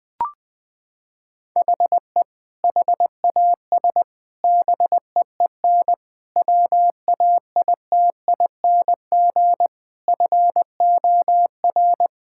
Morse code, heard as a keyed tone, about 20 wpm, 700 Hz. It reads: HE HAS BEEN WAITING FOR
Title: Sentences with 3-Word Phrases - 20wpm